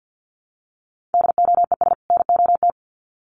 Morse code, heard as a keyed tone, about 50 wpm, 700 Hz. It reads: DOES NOT
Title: Sets of 2 Words 50wpm